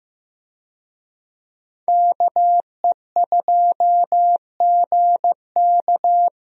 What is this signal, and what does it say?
Morse code, heard as a keyed tone, about 16 wpm, 700 Hz. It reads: KE2GK
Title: USA Call Signs - 5 Characters - Repeated 3x - 15wpm